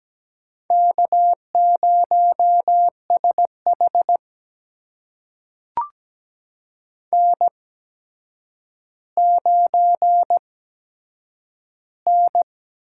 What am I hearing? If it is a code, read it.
Morse code, heard as a keyed tone, about 17 wpm, spaced as for 8 wpm, 700 Hz. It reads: K0SH N9N
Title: USA Call Signs - 4 Characters - 8x Character Spacing - 17wpm